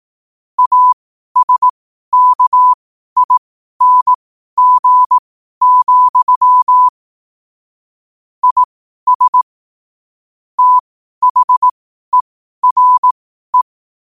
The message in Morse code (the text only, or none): ASKING, IS THERE